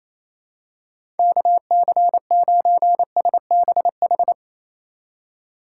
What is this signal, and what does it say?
Morse code, heard as a keyed tone, about 28 wpm, 700 Hz. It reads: KC9SBH
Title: USA Call Signs - 6 Characters - 2x Character Spacing - 28wpm